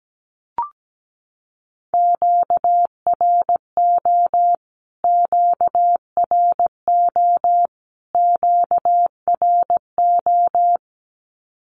Morse code, heard as a keyed tone, about 17 wpm, 700 Hz. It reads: QRO QRO QRO
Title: Q-Codes Repeated 3x 17wpm